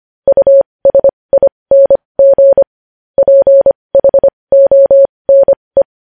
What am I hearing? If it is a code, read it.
Morse code, heard as a keyed tone, about 25 wpm, spaced as for 20 wpm, 550 Hz. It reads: USING PHONE
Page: News Headlines in Morse Code at 20 WPM